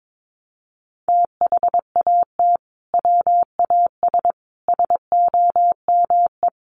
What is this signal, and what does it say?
Morse code, heard as a keyed tone, about 22 wpm, 700 Hz. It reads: THAT WAS SOME
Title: Sentences from Top 1400 Words 22wpm